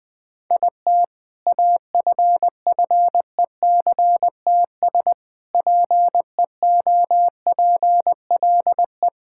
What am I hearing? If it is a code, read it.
Morse code, heard as a keyed tone, about 20 wpm, 700 Hz. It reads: IT AFFECTS PEOPLE
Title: Sentences with 7-Word Phrases - 20wpm